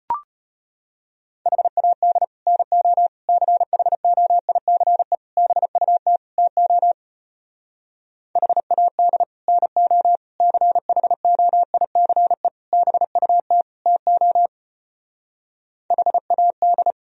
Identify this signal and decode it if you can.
Morse code, heard as a keyed tone, about 33 wpm, 700 Hz. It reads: HAD NO CHOICE BUT TO HAD NO CHOICE BUT TO HAD